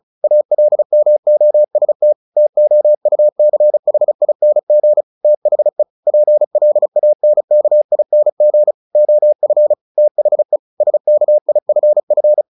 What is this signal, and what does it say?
Morse code, heard as a keyed tone, about 34 wpm, 600 Hz. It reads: ALMOST TOUCHING THE PLANKING OF THE SKIFF
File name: the old man and the sea 2 35wpm